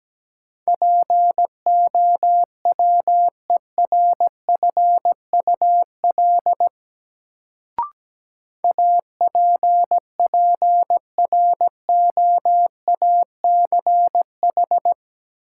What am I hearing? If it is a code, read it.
Morse code, heard as a keyed tone, about 17 wpm, 700 Hz. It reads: POWERFUL APPROACH
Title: Sentences from Top 1000 Words - Review Part 6 - New Word Drill - 17wpm